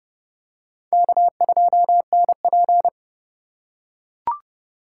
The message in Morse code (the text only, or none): K2NP